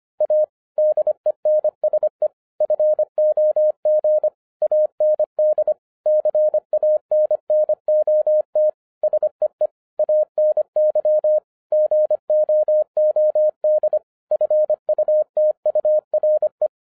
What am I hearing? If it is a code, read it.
Morse code, heard as a keyed tone, about 25 wpm, 610 Hz. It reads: A DENSE FOG AND CANNOT SEE ANY GOOD FUTURE